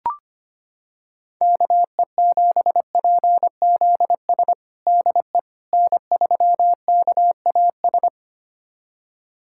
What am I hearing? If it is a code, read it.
Morse code, heard as a keyed tone, about 25 wpm, 700 Hz. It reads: KE7PZS DE N3KAS
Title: USA Callsign DE Callsign 25wpm